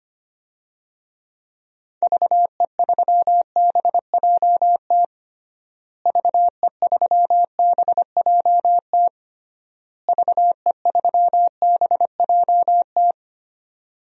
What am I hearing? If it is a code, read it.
Morse code, heard as a keyed tone, about 25 wpm, 700 Hz. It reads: VE3BJT VE3BJT VE3BJT
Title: DXCC-100 Call Signs - 6 Characters - Repeated 3x - 25wpm